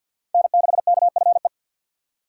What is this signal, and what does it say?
Morse code, heard as a keyed tone, about 47 wpm, 700 Hz. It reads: N6KFE
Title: USA Call Signs - 5 Characters - 6x Character Spacing - 50wpm